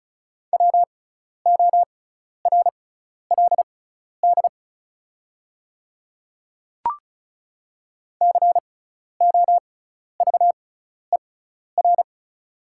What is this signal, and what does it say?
Morse code, heard as a keyed tone, about 34 wpm, spaced as for 12 wpm, 700 Hz. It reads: WORLD COVER
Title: 6x Character Spacing - Top 500 Words - 35wpm